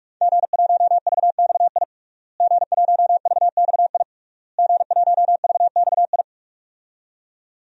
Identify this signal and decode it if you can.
Morse code, heard as a keyed tone, about 44 wpm, 700 Hz. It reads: G1VXI G1VXI G1VXI
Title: DXCC-100 Call Signs - 5 Characters - Repeated 3x - 45wpm